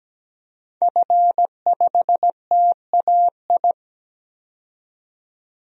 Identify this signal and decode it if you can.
Morse code, heard as a keyed tone, about 17 wpm, 700 Hz. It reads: F5TAI
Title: DXCC-100 Call Signs - 5 Characters - 17wpm